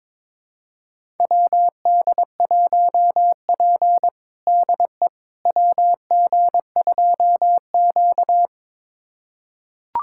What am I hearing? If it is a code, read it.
Morse code, heard as a keyed tone, about 22 wpm, 700 Hz. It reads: WD1P DE WG2Q
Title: USA Callsign DE Callsign - 4 Characters - 22wpm